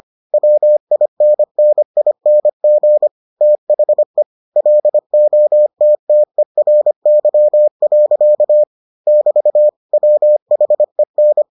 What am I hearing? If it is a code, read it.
Morse code, heard as a keyed tone, about 25 wpm, 600 Hz. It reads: WINNING THE LOTTERY. <BT> WHEN